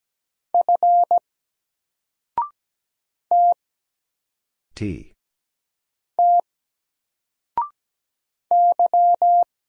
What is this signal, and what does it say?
Morse code, heard as a keyed tone, about 17 wpm, 700 Hz. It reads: F T T Y
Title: Single Character (Letter:Number:Punctuation:Prosign) 17wpm